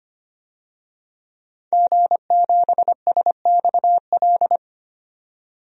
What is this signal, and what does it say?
Morse code, heard as a keyed tone, about 25 wpm, 700 Hz. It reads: G7SXL
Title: DXCC-100 Call Signs - 5 Characters - 25wpm